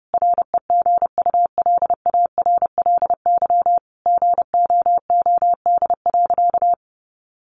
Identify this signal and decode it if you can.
Morse code, heard as a keyed tone, about 30 wpm, 700 Hz. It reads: REGULARLY GOOD.
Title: Sentences from Top 1400 Words 30wpm